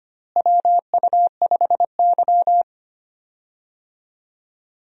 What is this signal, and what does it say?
Morse code, heard as a keyed tone, about 25 wpm, 700 Hz. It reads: WU5Y
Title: USA Call Signs - 4 Characters - Repeated 3x - 25wpm